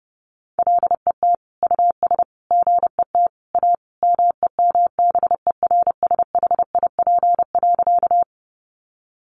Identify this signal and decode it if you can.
Morse code, heard as a keyed tone, about 30 wpm, 700 Hz. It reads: LET US GET A MEMBERSHIP.
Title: Sentences from Top 1400 Words 30wpm